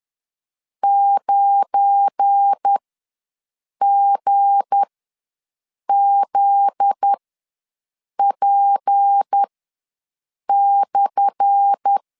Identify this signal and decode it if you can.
Morse code, heard as a keyed tone, about 10 wpm, spaced as for 6 wpm, 800 Hz. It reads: 9GZP/
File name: LEZIONI / 10 SIGNS / 10A SLASH ws 10wpm 4121